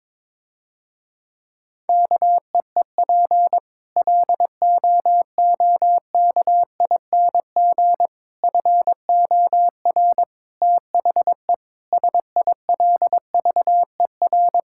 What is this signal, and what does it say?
Morse code, heard as a keyed tone, about 22 wpm, 700 Hz. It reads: KEEP LOOKING FOR THE SILVER
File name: Difficult Sentences with Hints - 22wpm